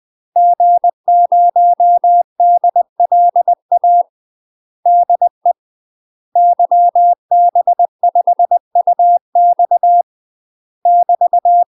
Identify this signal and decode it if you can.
Morse code, heard as a keyed tone, about 20 wpm, 700 Hz. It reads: G0DLA DE YB5UX <BT>
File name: UK MASTER QSOs TRAINING / 20WPM / UK MASTER QSOs 0034